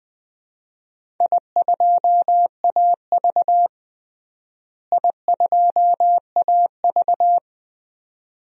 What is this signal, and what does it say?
Morse code, heard as a keyed tone, about 20 wpm, 700 Hz. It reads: I2AV I2AV
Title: DXCC-100 Call Signs - 4 Characters - Repeated 2x - 20wpm